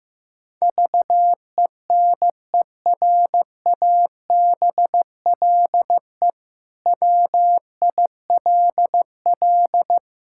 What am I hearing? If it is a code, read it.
Morse code, heard as a keyed tone, about 16 wpm, 700 Hz. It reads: VENERABLE WILL
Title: Pangram Challenge v2 - 15wpm